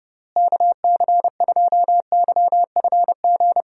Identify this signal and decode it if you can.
Morse code, heard as a keyed tone, about 30 wpm, 700 Hz. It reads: KC2YFG
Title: USA Call Signs - 6 Characters - Repeated 2x - 30wpm